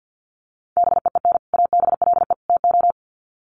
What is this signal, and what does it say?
Morse code, heard as a keyed tone, about 50 wpm, 700 Hz. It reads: BEEN ABLE TO
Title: Sets of 3 Words 50wpm